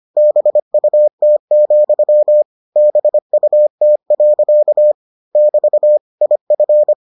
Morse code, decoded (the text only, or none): BUT, BUT. <BT> IF